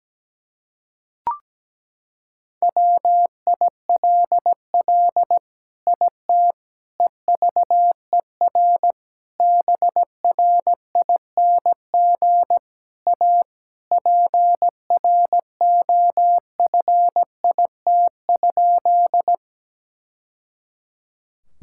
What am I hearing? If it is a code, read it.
Morse code, heard as a keyed tone, about 17 wpm, 700 Hz. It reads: WILL IT EVER BRING A PROFIT?
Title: Sentences from Top 1000 Words - Review Part 5 - 17wpm